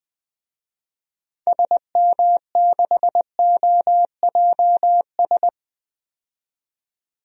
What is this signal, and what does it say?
Morse code, heard as a keyed tone, about 20 wpm, 700 Hz. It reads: SM6OJS
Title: DXCC-100 Call Signs - 6 Characters - 20wpm